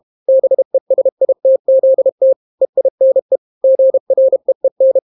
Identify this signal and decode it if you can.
Morse code, heard as a keyed tone, about 32 wpm, 530 Hz. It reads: BESITZT EINE GREEN